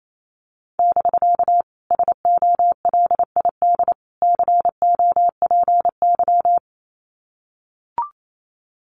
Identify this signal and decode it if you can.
Morse code, heard as a keyed tone, about 28 wpm, 700 Hz. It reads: <BK> SOLID COPY
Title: Common QSO Phrases Encore 28wpm